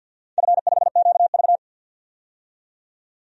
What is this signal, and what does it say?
Morse code, heard as a keyed tone, about 51 wpm, 700 Hz. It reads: F5QV
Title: DXCC-100 Call Signs - 4 Characters - 50wpm